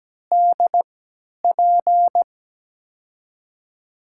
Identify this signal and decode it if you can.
Morse code, heard as a keyed tone, about 17 wpm, spaced as for 10 wpm, 700 Hz. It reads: DP
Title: USA Call Signs - 6 Characters - 3x Character Spacing - 17wpm